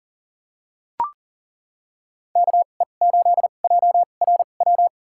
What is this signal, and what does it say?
Morse code, heard as a keyed tone, about 40 wpm, spaced as for 29 wpm, 700 Hz. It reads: KE8JRW
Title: USA Call Signs - 2x Character Spacing - 40wpm